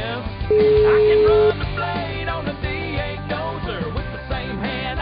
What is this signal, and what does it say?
Ringback tone (Europe, 425 Hz), with noise.